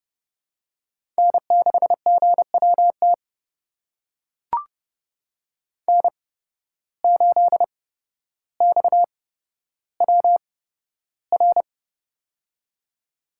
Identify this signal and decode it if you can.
Morse code, heard as a keyed tone, about 30 wpm, spaced as for 13 wpm, 700 Hz. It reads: N6GWT N8XWR
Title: USA Call Signs - 5 Characters - 8x Character Spacing - 30wpm